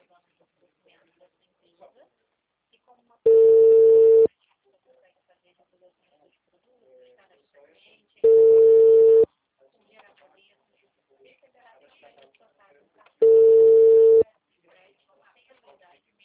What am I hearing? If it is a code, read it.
Ringback tone (China, 450 Hz), with speech.